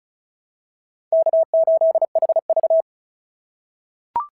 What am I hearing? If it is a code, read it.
Morse code, heard as a keyed tone, about 36 wpm, 690 Hz. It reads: K8HV
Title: SOTA QSO Practice 35wpm